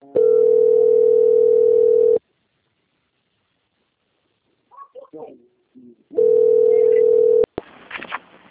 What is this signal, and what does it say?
Ringback tone (North America), 440 + 480 Hz, with speech.
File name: out- - - - - .36160